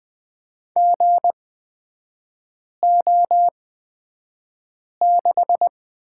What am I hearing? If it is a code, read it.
Morse code, heard as a keyed tone, about 20 wpm, spaced as for 5 wpm, 700 Hz. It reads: GO6
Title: 243 Lesson 39e - Letter z - Three Characters 20wpm